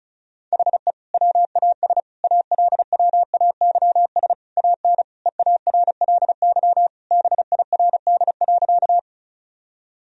Morse code, heard as a keyed tone, about 36 wpm, 700 Hz. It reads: HE WAS ALWAYS AN EARLY BIRD.
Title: Sentences from Top 1000 Words - Review Part 2 - 35wpm